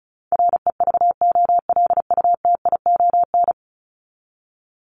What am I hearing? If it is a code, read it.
Morse code, heard as a keyed tone, about 34 wpm, 700 Hz. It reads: REVOLUTION
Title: Top 1751-2000 Words 35wpm